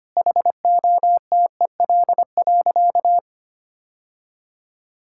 Morse code, heard as a keyed tone, about 25 wpm, 700 Hz. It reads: HOTEL.
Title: Sentences from Top 1600 Words 25wpm